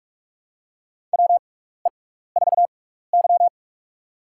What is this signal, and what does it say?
Morse code, heard as a keyed tone, about 46 wpm, spaced as for 16 wpm, 700 Hz. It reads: WE4Y
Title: USA Call Signs - 4 Characters - 6x Character Spacing - 45wpm